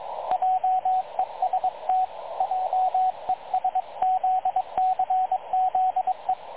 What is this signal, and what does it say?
Morse code, heard as a keyed tone, about 22 wpm, 750 Hz, with radio noise. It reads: JEST JESZCZE
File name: dlazaawansowanych / nagraniazpasma / websdr recording 3567.0kHz